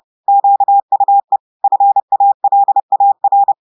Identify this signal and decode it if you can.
Morse code, heard as a keyed tone, about 30 wpm, 800 Hz. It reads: QUE FALAR
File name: audios / portugais / portugais texte en clair 2 vitesse 30 mots minutes